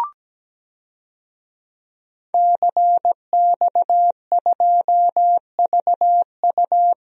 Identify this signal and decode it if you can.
Morse code, heard as a keyed tone, about 17 wpm, 700 Hz. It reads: CX2VU
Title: DXCC-100 Call Signs - 5 Characters - 17wpm